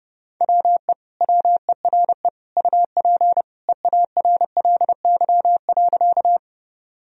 Morse code, heard as a keyed tone, about 32 wpm, 700 Hz. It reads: WE WERE UP EARLY.
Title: Sentences from Top 1000 Words - Review Part 2 - 30wpm